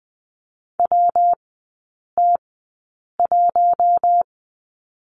Morse code, heard as a keyed tone, about 20 wpm, spaced as for 8 wpm, 700 Hz. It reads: WT1
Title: 114 Lesson 19e - Letter w - Three Characters 20wpm